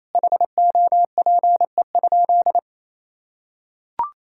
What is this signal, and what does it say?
Morse code, heard as a keyed tone, about 27 wpm, 700 Hz. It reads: HOPE?
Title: Sentences from Top 1000 Words - Review Part 5 - 28wpm